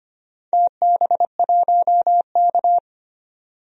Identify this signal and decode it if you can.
Morse code, heard as a keyed tone, about 25 wpm, 700 Hz. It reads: TB1K
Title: DXCC-100 Call Signs - 4 Characters - 25wpm